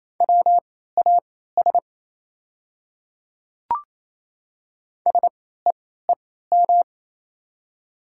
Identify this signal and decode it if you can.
Morse code, heard as a keyed tone, about 28 wpm, spaced as for 16 wpm, 700 Hz. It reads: WAS SEEM
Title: 3x Character Spacing - Top 300 Words - 28wpm